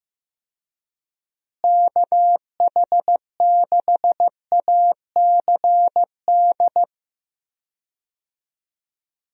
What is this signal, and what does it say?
Morse code, heard as a keyed tone, about 15 wpm, 700 Hz. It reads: KH6ACD
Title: USA Call Signs - 6 Characters - Repeated 3x - 15wpm